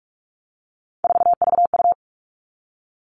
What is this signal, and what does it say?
Morse code, heard as a keyed tone, about 44 wpm, 700 Hz. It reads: 4VU
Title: All Cut Numbers - 3-Digit 45wpm